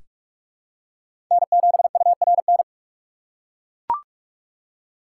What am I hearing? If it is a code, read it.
Morse code, heard as a keyed tone, about 43 wpm, 700 Hz. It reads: N7URN
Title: USA Call Signs - 5 Characters - 6x Character Spacing - 45wpm